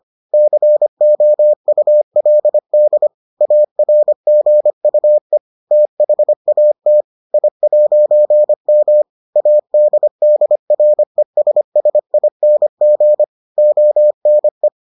Morse code, heard as a keyed tone, about 25 wpm, 600 Hz. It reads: COULD ARGUE THAT I'M ADDRESSING ONE